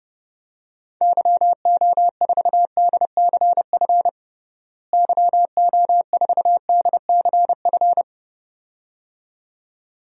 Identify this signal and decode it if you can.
Morse code, heard as a keyed tone, about 30 wpm, 700 Hz. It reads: YO4DCF YO4DCF
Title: DXCC-100 Call Signs - 6 Characters - Repeated 2x - 30wpm